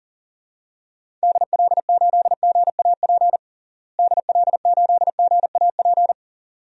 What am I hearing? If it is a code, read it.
Morse code, heard as a keyed tone, about 40 wpm, 700 Hz. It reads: DL8GAP DL8GAP
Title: DXCC-100 Call Signs - 6 Characters - Repeated 2x - 40wpm